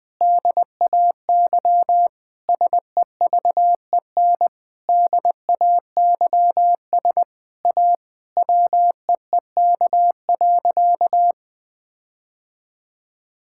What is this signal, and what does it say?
Morse code, heard as a keyed tone, about 20 wpm, 700 Hz. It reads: DAY SEVEN DAYS A WEEK.
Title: Sentences with 7-Word Phrases - 20wpm